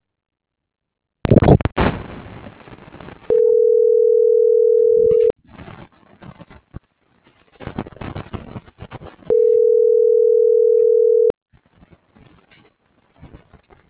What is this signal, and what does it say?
Ringback tone (North America), 440 + 480 Hz.